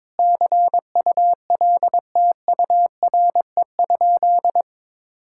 Morse code, heard as a keyed tone, about 21 wpm, 700 Hz. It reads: CULTURE?
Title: Sentences from Top 1000 Words - Review Part 6 - 22wpm